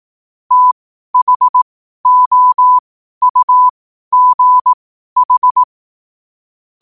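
Morse code, heard as a keyed tone, about 18 wpm, spaced as for 12 wpm, 1000 Hz. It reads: THOUGH
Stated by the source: Morse question-of-the-day sent at 12 wpm.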